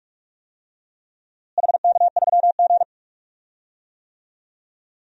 Morse code, heard as a keyed tone, about 46 wpm, 700 Hz. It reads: HK3G
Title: DXCC-100 Call Signs - 4 Characters - Repeated 3x - 45wpm